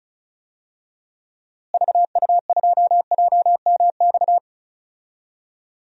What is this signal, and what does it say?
Morse code, heard as a keyed tone, about 34 wpm, 700 Hz. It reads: VU2JMX
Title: DXCC-100 Call Signs - 6 Characters - Repeated 3x - 35wpm